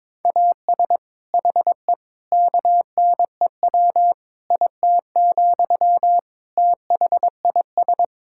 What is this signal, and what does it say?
Morse code, heard as a keyed tone, about 22 wpm, 700 Hz. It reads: AS HE KNEW IT, THIS
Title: Sentences from Top 1600 Words 22wpm